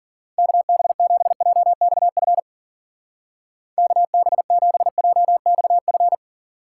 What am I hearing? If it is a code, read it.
Morse code, heard as a keyed tone, about 50 wpm, 700 Hz. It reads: KB7JXF KB7JXF
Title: Speed Racing - USA Call Signs 40wpm